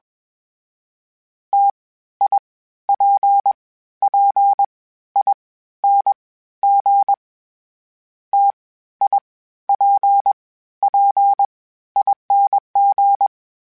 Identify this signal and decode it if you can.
Morse code, heard as a keyed tone, about 21 wpm, 800 Hz. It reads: T I P P I N G T I P P ING
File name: Variable Speed / Variable Speed-mp3s / T AOEI's sp @20